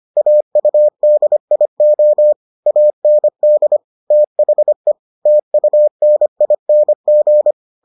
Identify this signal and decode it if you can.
Morse code, heard as a keyed tone, about 25 wpm, 600 Hz. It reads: AUDIO AND THE TUNING